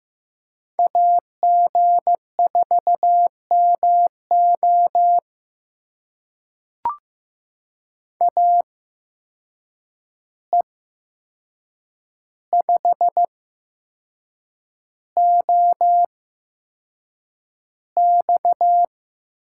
Morse code, heard as a keyed tone, about 15 wpm, spaced as for 6 wpm, 700 Hz. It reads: AG4MO AE5OX
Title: USA Call Signs - 8x Character Spacing - 15wpm